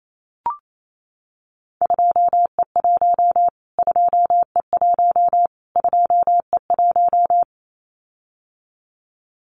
Morse code, heard as a keyed tone, about 28 wpm, 700 Hz. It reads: 2E1 2E1 2E1
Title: Callsign Prefixes for Top 25 DXCC - Repeated 3x - 28wpm